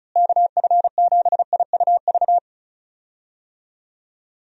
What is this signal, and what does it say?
Morse code, heard as a keyed tone, about 34 wpm, 700 Hz. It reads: KF7IUV
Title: USA Call Signs - 6 Characters - Repeated 2x - 35wpm